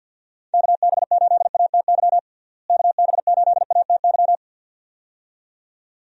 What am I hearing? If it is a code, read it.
Morse code, heard as a keyed tone, about 53 wpm, 700 Hz. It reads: KB8ATY KB8ATY
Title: USA Call Signs - 6 Characters - Repeated 2x - 50wpm